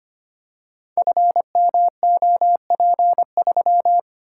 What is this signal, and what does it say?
Morse code, heard as a keyed tone, about 25 wpm, 700 Hz. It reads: FMOP3
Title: Code Groups - 5 Letter:Numbers - 25wpm